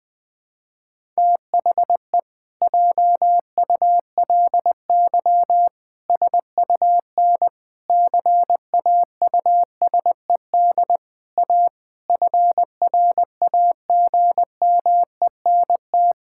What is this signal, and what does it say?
Morse code, heard as a keyed tone, about 20 wpm, 700 Hz. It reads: THE JULY SUN CAUSED A FRAGMENT
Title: Pangram Challenge v2 - 20wpm